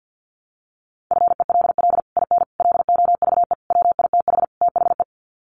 Morse code, heard as a keyed tone, about 50 wpm, 700 Hz. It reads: FELL IN LOVE WITH THE
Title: Sets of 5 Words Encore2 50wpm